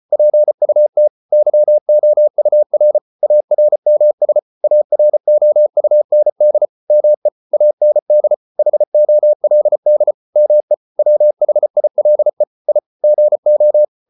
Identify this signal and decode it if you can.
Morse code, heard as a keyed tone, about 34 wpm, 600 Hz. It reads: PUT YOUR ARMS AROUND ME AND HOLD ME WHILE I GO